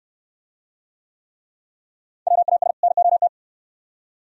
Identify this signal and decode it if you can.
Morse code, heard as a keyed tone, about 64 wpm, 700 Hz. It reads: WAS NOT